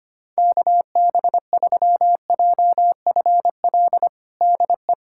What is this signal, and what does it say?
Morse code, heard as a keyed tone, about 25 wpm, 700 Hz. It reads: KB3JFL DE
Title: USA Callsign DE Callsign 25wpm